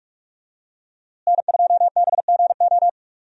Morse code, heard as a keyed tone, about 42 wpm, 700 Hz. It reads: N2BGO
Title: USA Call Signs - 5 Characters - Repeated 3x - 45wpm